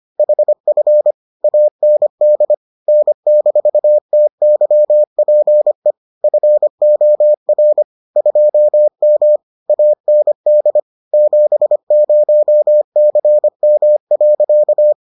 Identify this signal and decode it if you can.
Morse code, heard as a keyed tone, about 25 wpm, 600 Hz. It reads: HF AND N-TYPE FOR 2M AND 70CM.